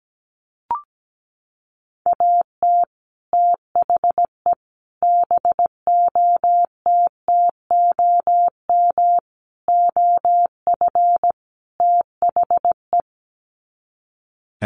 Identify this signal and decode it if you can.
Morse code, heard as a keyed tone, about 17 wpm, 700 Hz. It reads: AT THE BOTTOM OF THE
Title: Sets of 5 Words 17wpm